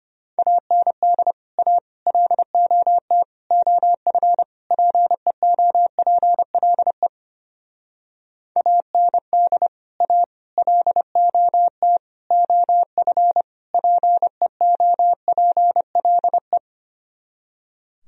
Morse code, heard as a keyed tone, about 31 wpm, 700 Hz. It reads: AND A LOT OF PEOPLE AND A LOT OF PEOPLE
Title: Speed-Racing - Sets of 5 Words 25wpm